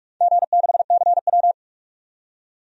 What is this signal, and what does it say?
Morse code, heard as a keyed tone, about 44 wpm, 700 Hz. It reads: G6CW
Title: DXCC-100 Call Signs - 4 Characters - 45wpm